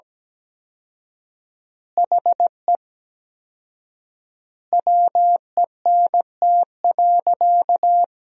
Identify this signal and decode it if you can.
Morse code, heard as a keyed tone, about 17 wpm, 700 Hz. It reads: HE WENT.
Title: 4x Word-Spacing - Sentences from Top 200 Words - 17wpm